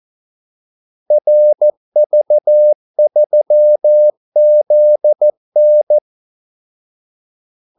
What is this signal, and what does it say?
Morse code, heard as a keyed tone, about 14 wpm, 600 Hz. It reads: RV3ZN
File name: Callsigns-14wpm